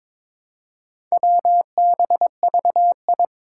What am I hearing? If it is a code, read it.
Morse code, heard as a keyed tone, about 22 wpm, 700 Hz. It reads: WBVI
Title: Four Challenging Letter-Number Combinations 22wpm